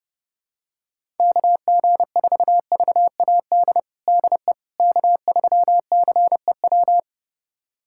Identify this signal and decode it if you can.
Morse code, heard as a keyed tone, about 31 wpm, 700 Hz. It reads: KG4VAD DE K3CEW
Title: USA Callsign DE Callsign 30wpm